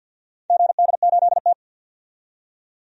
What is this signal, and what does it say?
Morse code, heard as a keyed tone, about 53 wpm, 700 Hz. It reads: GD8T
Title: DXCC-100 Call Signs - 4 Characters - 50wpm